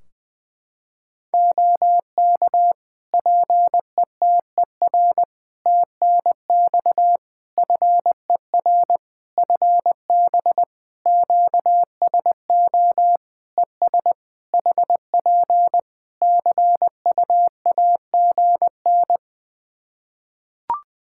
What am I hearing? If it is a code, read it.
Morse code, heard as a keyed tone, about 20 wpm, 700 Hz. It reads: OK PETER TNX FER FB QSO ES HP CUAGN
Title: Practice QSO - Exchange 4 - Ending - 20wpm